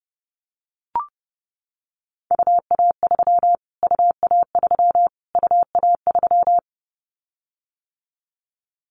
Morse code, heard as a keyed tone, about 30 wpm, 700 Hz. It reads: UA3 UA3 UA3
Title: Callsign Prefixes for Top 25 DXCC - Repeated 3x - 30wpm